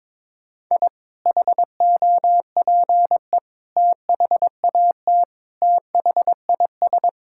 Morse code, heard as a keyed tone, about 21 wpm, 700 Hz. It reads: I HOPE THAT THIS
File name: practice / 22wpm / Sentences with 3-Word Phrases - 22wpm